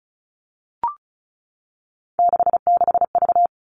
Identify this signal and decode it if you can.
Morse code, heard as a keyed tone, about 34 wpm, 700 Hz. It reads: 66V
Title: All Cut Numbers - 3-Digit 35wpm